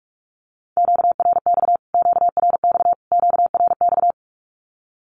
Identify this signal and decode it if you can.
Morse code, heard as a keyed tone, about 46 wpm, 700 Hz. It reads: QRX QRX QRX
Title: Q-Codes Repeated 3x 45wpm